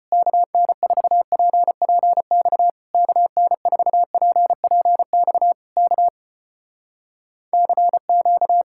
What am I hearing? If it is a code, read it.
Morse code, heard as a keyed tone, about 36 wpm, 700 Hz. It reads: KN4PPX KN4PPX K CQ